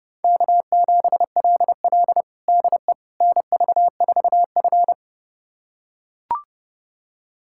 Morse code, heard as a keyed tone, about 30 wpm, 700 Hz. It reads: K7LL DE NV4F
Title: USA Callsign DE Callsign - 4 Characters - 30wpm